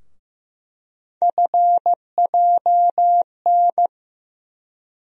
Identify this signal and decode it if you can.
Morse code, heard as a keyed tone, about 15 wpm, 700 Hz. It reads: FJN
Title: USA Call Sign Postfixes - 3 Letters 15wpm